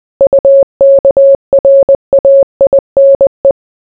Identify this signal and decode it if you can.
Morse code, heard as a keyed tone, about 20 wpm, 550 Hz. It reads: UKRAINE